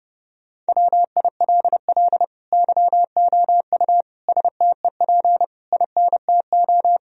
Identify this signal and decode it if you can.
Morse code, heard as a keyed tone, about 30 wpm, 700 Hz. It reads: WILL YOU STEP INTO
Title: Sentences from Top 1000 Words - Review Part 2 - 30wpm